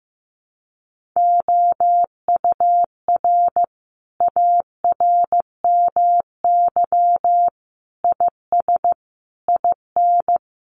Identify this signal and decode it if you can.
Morse code, heard as a keyed tone, about 15 wpm, 700 Hz. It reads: OUR ARMY IS IN
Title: Sentences from Top 1500 Words 15wpm